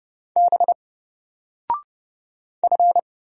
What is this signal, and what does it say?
Morse code, heard as a keyed tone, about 29 wpm, 700 Hz. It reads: B F
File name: Single-Letter-Number-30wpm